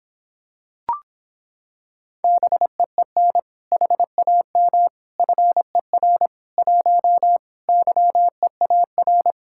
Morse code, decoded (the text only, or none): BEEN HAM FER 1 YEAR